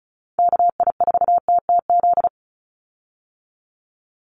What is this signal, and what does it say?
Morse code, heard as a keyed tone, about 34 wpm, 700 Hz. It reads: KI4TTZ
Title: USA Call Signs 35wpm